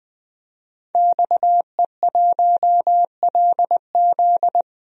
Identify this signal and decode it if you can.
Morse code, heard as a keyed tone, about 20 wpm, 700 Hz. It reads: XE1LZ
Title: DXCC-100 Call Signs - 5 Characters - Repeated 2x - 20wpm